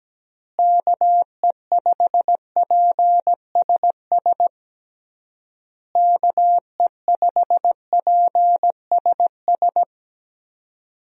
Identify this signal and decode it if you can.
Morse code, heard as a keyed tone, about 17 wpm, 700 Hz. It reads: KE5PSS KE5PSS
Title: USA Call Signs - 6 Characters - Repeated 3x - 17wpm